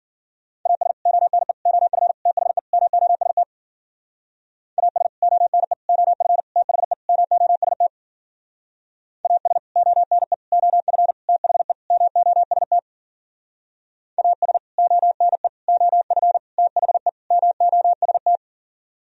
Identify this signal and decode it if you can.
Morse code, heard as a keyed tone, about 52 wpm, 700 Hz. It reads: AS ONE OF THE MOST AS ONE OF THE MOST AS ONE OF THE MOST AS ONE OF THE MOST